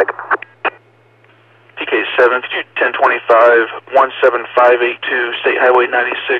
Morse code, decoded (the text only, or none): none